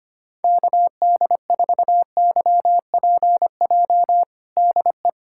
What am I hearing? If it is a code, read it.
Morse code, heard as a keyed tone, about 25 wpm, 700 Hz. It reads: KD4YPJ DE
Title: USA Callsign DE Callsign - 6 Characters - 25wpm